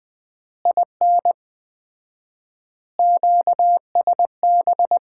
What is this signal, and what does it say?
Morse code, heard as a keyed tone, about 21 wpm, 700 Hz. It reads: IN QSB
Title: Common QSO Phrases - 4x Word-Spacing - 20wpm